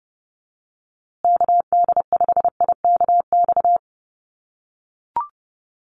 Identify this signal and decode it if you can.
Morse code, heard as a keyed tone, about 30 wpm, 700 Hz. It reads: KD5IKX
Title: USA Call Signs Repeated 2x 30wpm